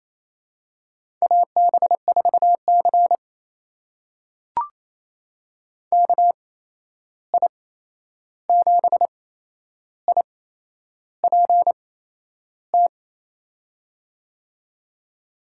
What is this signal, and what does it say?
Morse code, heard as a keyed tone, about 28 wpm, spaced as for 11 wpm, 700 Hz. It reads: AB4C KI7IPT
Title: USA Call Signs - 8x Character Spacing - 28wpm